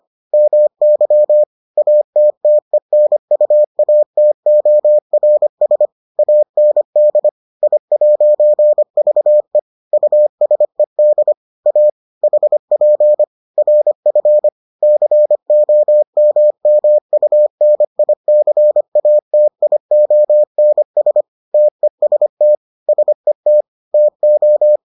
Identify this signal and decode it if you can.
Morse code, heard as a keyed tone, about 25 wpm, 600 Hz. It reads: MY ATTENUATORS AND I'VE USED A HP RF COMMUNICATIONS TEST SET TO